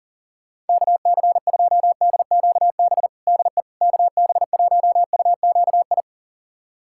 Morse code, heard as a keyed tone, about 40 wpm, 700 Hz. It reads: KC2DQB DE KB1UQI
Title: USA Callsign DE Callsign 40wpm